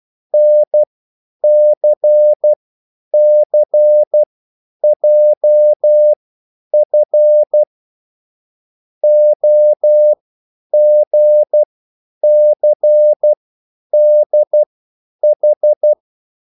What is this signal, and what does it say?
Morse code, heard as a keyed tone, about 12 wpm, spaced as for 9 wpm, 600 Hz. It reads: NCCJF OGCDH